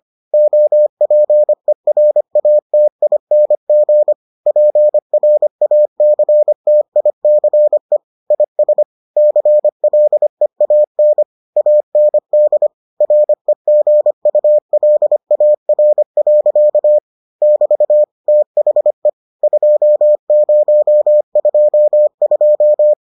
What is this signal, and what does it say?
Morse code, heard as a keyed tone, about 25 wpm, 600 Hz. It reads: OPERATING PRACTICE IS CLEAN AND REGULAR. <BT> THE 2022